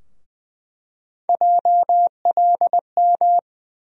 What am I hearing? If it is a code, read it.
Morse code, heard as a keyed tone, about 20 wpm, 700 Hz. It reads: JLM
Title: USA Call Sign Postfixes - 3 Letters 20wpm